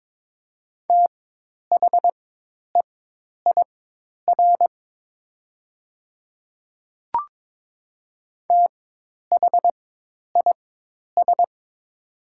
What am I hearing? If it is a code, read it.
Morse code, heard as a keyed tone, about 22 wpm, spaced as for 10 wpm, 700 Hz. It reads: THEIR THIS
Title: 4x Character Spacing - Top 100 Words - 22wpm